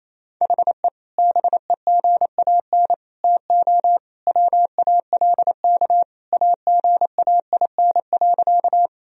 Morse code, heard as a keyed tone, about 28 wpm, 700 Hz. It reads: HE BEGAN TO WALK AGAIN.
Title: Sentences from Top 1000 Words - Review Part 1 - 28wpm